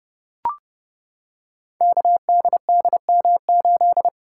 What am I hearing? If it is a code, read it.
Morse code, heard as a keyed tone, about 30 wpm, 700 Hz. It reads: KDDM8
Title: Code Groups - 5 Letter:Numbers - 30wpm